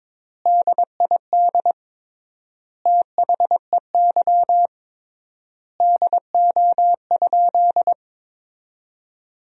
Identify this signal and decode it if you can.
Morse code, heard as a keyed tone, about 22 wpm, 700 Hz. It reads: DID THEY DO?
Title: 3x Word-Spacing - Sentences from Top 100 Words - 22wpm